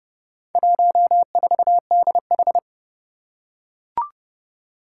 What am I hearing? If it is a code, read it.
Morse code, heard as a keyed tone, about 30 wpm, 700 Hz. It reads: 14DH
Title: Four Challenging Letter-Number Combinations 30wpm